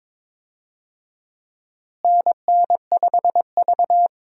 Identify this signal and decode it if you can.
Morse code, heard as a keyed tone, about 22 wpm, 700 Hz. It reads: NN5V
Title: USA Call Signs - 4 Characters - Repeated 3x - 22wpm